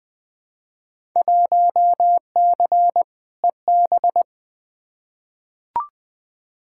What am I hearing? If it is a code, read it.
Morse code, heard as a keyed tone, about 20 wpm, 700 Hz. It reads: 1C EB
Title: Field Day Exchanges Encore - 20wpm